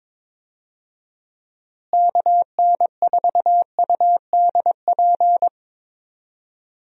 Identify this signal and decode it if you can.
Morse code, heard as a keyed tone, about 22 wpm, 700 Hz. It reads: KN4UDP
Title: USA Call Signs - 6 Characters - Repeated 3x - 22wpm